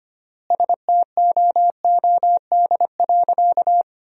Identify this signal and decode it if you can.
Morse code, heard as a keyed tone, about 25 wpm, 700 Hz. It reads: STOOD.
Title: 3x Word-Spacing - Sentences from Top 500 Words - 25wpm